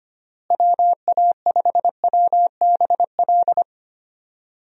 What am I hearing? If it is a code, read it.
Morse code, heard as a keyed tone, about 25 wpm, 700 Hz. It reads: WA5WBL
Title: USA Call Signs - 6 Characters - Repeated 3x - 25wpm